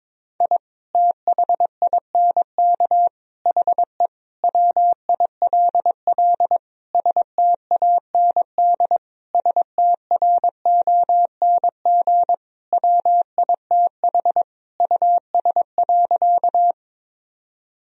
Morse code, heard as a keyed tone, about 21 wpm, 700 Hz. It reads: I THINK HE WILL STAND STRONG WITH US.
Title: Sentences from Top 1000 Words - Review Part 3 - 22wpm